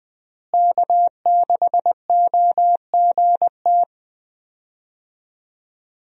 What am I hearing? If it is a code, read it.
Morse code, heard as a keyed tone, about 20 wpm, 700 Hz. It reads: K6OGT
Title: USA Call Signs - 5 Characters - Repeated 3x - 20wpm